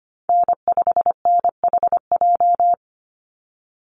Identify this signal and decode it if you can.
Morse code, heard as a keyed tone, about 25 wpm, 700 Hz. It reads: N5NHJ
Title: Call Signs Repeated 3x - 25wpm